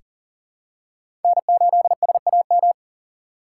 Morse code, heard as a keyed tone, about 38 wpm, 700 Hz. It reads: N8SAM
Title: USA Call Signs - 5 Characters - 6x Character Spacing - 40wpm